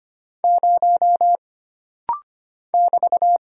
Morse code, heard as a keyed tone, about 25 wpm, 700 Hz. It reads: 0 <BT>
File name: Single Character (Letter:Number:Punctuation:Prosign) - Rapid-Fire - 25wpm